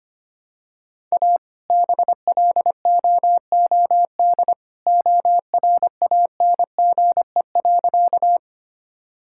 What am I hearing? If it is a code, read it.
Morse code, heard as a keyed tone, about 25 wpm, 700 Hz. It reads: A BLOOD ORANGE.
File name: practice / 25wpm / All-Age Appropriate Jokes - 25wpm